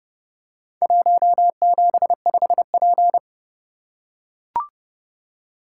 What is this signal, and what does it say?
Morse code, heard as a keyed tone, about 29 wpm, 700 Hz. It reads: 175P